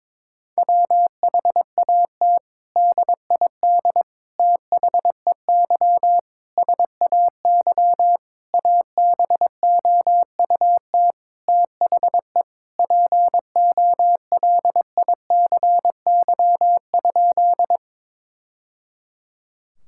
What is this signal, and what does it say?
Morse code, heard as a keyed tone, about 21 wpm, 700 Hz. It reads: WHAT DID THEY SAY ABOUT THE POLICY?
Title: Sentences from Top 1000 Words - Review Part 5 - 22wpm